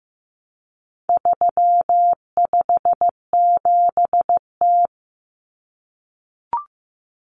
Morse code, heard as a keyed tone, about 15 wpm, 700 Hz. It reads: 357T
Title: Common Cut Numbers - 4-Digit 15wpm